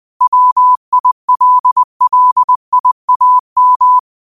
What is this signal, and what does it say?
Morse code, heard as a keyed tone, about 20 wpm, 1000 Hz. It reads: WILLIAM
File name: qotdfiles / 20wpm